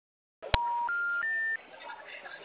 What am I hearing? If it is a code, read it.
Special information tone, with noise.